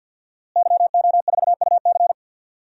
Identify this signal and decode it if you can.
Morse code, heard as a keyed tone, about 53 wpm, 700 Hz. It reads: YO4UC
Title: DXCC-100 Call Signs - 5 Characters - Repeated 2x - 50wpm